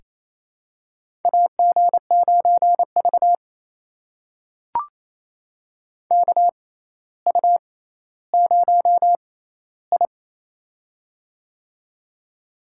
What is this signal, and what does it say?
Morse code, heard as a keyed tone, about 29 wpm, spaced as for 14 wpm, 700 Hz. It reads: AG9V KU0I
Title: USA Call Signs - 4 Characters - 6x Character Spacing - 28wpm